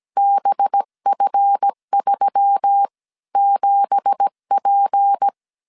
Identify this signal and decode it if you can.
Morse code, heard as a keyed tone, about 17 wpm, spaced as for 14 wpm, 800 Hz. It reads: BF37P
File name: 1500C S B 16wpm